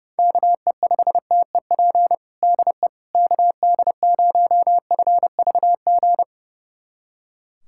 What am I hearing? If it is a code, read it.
Morse code, heard as a keyed tone, about 30 wpm, 690 Hz. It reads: KE5TEP DE KD0FVG
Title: USA Callsign DE Callsign - 6 Characters - 30wpm